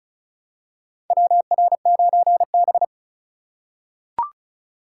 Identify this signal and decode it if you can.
Morse code, heard as a keyed tone, about 36 wpm, 700 Hz. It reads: WR9B
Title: USA Call Signs - 4 Characters - 6x Character Spacing - 35wpm